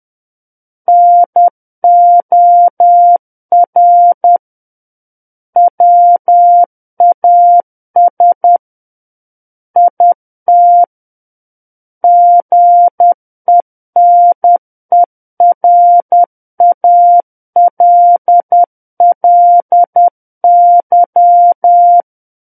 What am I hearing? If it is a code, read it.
Morse code, heard as a keyed tone, about 10 wpm, 700 Hz. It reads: NOR WAS IT GENERALLY
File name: Book One / 01-Chapter 1 - 10 WPM